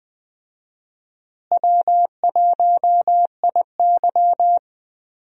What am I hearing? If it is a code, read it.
Morse code, heard as a keyed tone, about 20 wpm, 700 Hz. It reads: W1IY
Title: USA Call Signs - 4 Characters - Repeated 2x - 20wpm